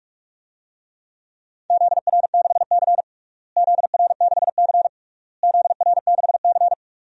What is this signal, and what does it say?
Morse code, heard as a keyed tone, about 44 wpm, 700 Hz. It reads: ZR6C ZR6C ZR6C
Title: DXCC-100 Call Signs - 4 Characters - Repeated 3x - 45wpm